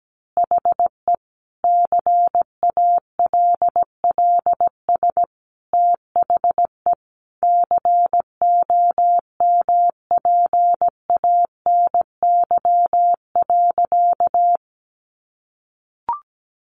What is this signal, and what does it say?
Morse code, heard as a keyed tone, about 17 wpm, 700 Hz. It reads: HE CALLS THE COMPANY.
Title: Additional Easy Phrases 17wpm